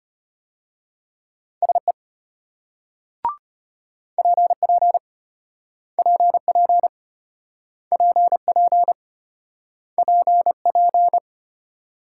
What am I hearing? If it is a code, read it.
Morse code, heard as a keyed tone, about 33 wpm, 700 Hz. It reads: SE PP PP PP PP